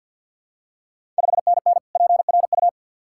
Morse code, heard as a keyed tone, about 53 wpm, 690 Hz. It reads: 5NN PRU
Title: Ontario QSO Party Exchange 50wpm